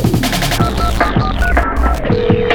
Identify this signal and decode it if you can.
There is no code.